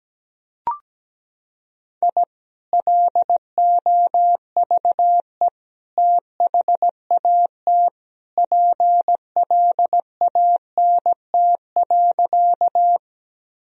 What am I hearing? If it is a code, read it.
Morse code, heard as a keyed tone, about 17 wpm, 700 Hz. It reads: I LOVE THAT PLANT.
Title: Sentences from Top 1000 Words - Review Part 1 - 17wpm